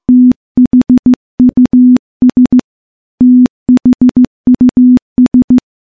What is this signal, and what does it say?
Morse code, heard as a keyed tone, about 15 wpm, 250 Hz. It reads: THUS THUS
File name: morse-training-15WPM